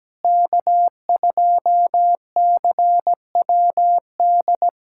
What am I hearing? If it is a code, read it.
Morse code, heard as a keyed tone, about 17 wpm, 700 Hz. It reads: K2CWD
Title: USA Callsign DE Callsign - 5 Characters - 17wpm